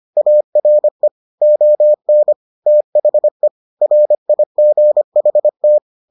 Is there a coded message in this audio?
Yes: Morse code from 0.2 to 5.8 s, heard as a keyed tone, about 25 wpm, 600 Hz. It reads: ARE ON THE RIGHT